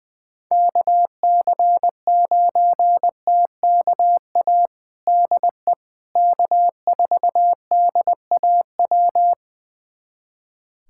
Morse code, heard as a keyed tone, about 21 wpm, 700 Hz. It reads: KC9TKA DE K4DAW
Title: USA Callsign DE Callsign 20wpm